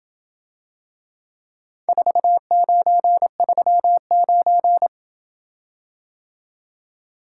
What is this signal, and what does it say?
Morse code, heard as a keyed tone, about 27 wpm, 700 Hz. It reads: 4939